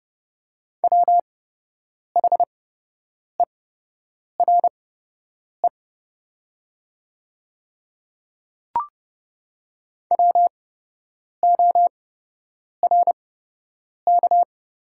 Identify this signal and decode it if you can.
Morse code, heard as a keyed tone, about 30 wpm, spaced as for 8 wpm, 700 Hz. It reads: WHERE WORK
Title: 8x Character Spacing - Top 200 Words - 30wpm